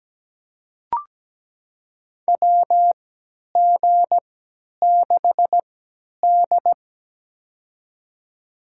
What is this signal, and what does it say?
Morse code, heard as a keyed tone, about 17 wpm, spaced as for 10 wpm, 700 Hz. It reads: WG6D
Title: USA Call Signs - 4 Characters - 3x Character Spacing - 17wpm